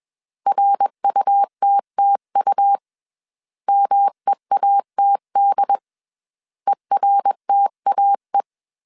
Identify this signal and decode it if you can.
Morse code, heard as a keyed tone, about 21 wpm, 800 Hz. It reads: RUTTU MEATD ERTAE